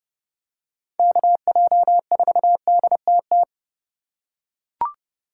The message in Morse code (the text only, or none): KJ4DTT